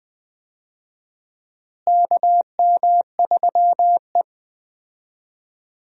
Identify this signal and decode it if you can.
Morse code, heard as a keyed tone, about 20 wpm, 700 Hz. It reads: KM3E
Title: USA Call Signs - 4 Characters - Repeated 3x - 20wpm